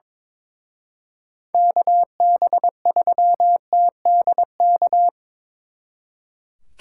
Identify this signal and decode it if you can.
Morse code, heard as a keyed tone, about 25 wpm, 700 Hz. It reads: KB3TDK
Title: Speed Racing - USA 2x3 Call Signs 22wpm